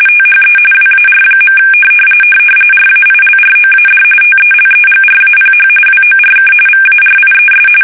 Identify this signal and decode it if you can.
RTTY, about 45.45 baud. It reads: THE WAR. RAILROAD TELEGRAPH WORK VARIES
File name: fsk850